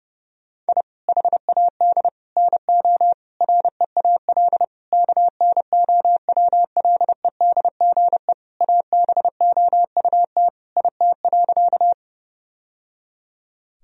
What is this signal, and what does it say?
Morse code, heard as a keyed tone, about 30 wpm, 700 Hz. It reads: I HAD NO REAL KNOWLEDGE ABOUT IT.
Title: Sentences from Top 1000 Words - Review Part 3 - 30wpm